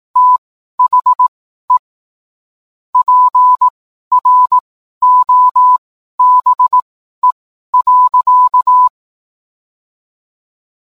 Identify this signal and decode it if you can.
Morse code, heard as a keyed tone, about 18 wpm, spaced as for 12 wpm, 1000 Hz. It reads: THE PROBE.